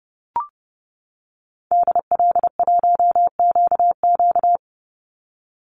Morse code, heard as a keyed tone, about 30 wpm, 700 Hz. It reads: DL1QQ
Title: Call Signs - 30wpm